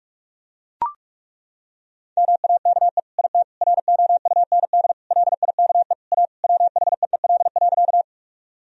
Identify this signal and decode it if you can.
Morse code, heard as a keyed tone, about 43 wpm, 700 Hz. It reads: MAKE IT ROUND LIKE A WHEEL.
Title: Sentences from Top 1000 Words - Review Part 3 - 45wpm